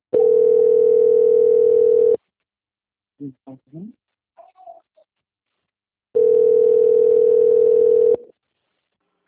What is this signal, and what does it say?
Ringback tone (North America), 440 + 480 Hz, with speech.
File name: out- - - - - .82139